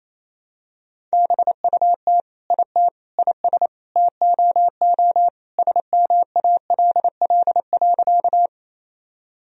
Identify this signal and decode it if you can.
Morse code, heard as a keyed tone, about 28 wpm, 700 Hz. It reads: BUT IT IS TOO SMALL.
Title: Sentences from Top 1000 Words - Review Part 1 - 28wpm